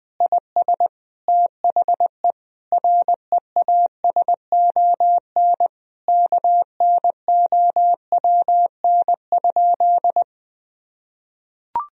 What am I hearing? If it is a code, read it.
Morse code, heard as a keyed tone, about 20 wpm, 700 Hz. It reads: IS THE REASON KNOWN?
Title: Sentences from Top 1000 Words - Review Part 3 - 20wpm